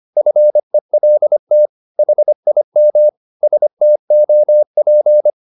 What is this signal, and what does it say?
Morse code, heard as a keyed tone, about 25 wpm, 600 Hz. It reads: FELT HIM STOP
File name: the old man and the sea part 1 25wpm